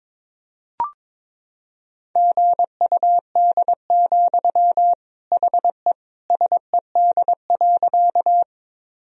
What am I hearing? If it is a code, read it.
Morse code, heard as a keyed tone, about 22 wpm, 700 Hz. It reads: GUD, HE SED.
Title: Sentences from Top 200 Words and Abbreviations 22wpm